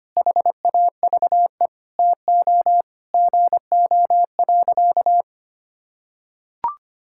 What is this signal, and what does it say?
Morse code, heard as a keyed tone, about 25 wpm, 700 Hz. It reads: HAVE TO GO.
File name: practice / 25wpm / Sentences with 4-Word Phrases - 25wpm